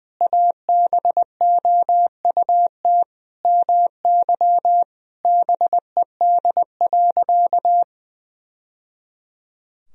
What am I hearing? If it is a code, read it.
Morse code, heard as a keyed tone, about 20 wpm, 700 Hz. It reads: ABOUT MY BED.
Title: Sentences from Top 1000 Words - Review Part 4 - 20wpm